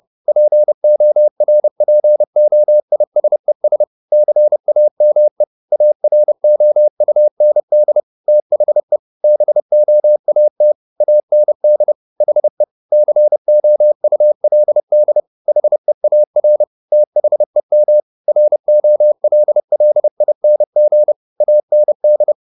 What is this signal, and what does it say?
Morse code, heard as a keyed tone, about 30 wpm, 600 Hz. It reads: PORPOISES CAME AROUND THE BOAT AND HE COULD HEAR THEM ROLLING AND
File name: the old man and the sea / the old man and the sea 1 30wpm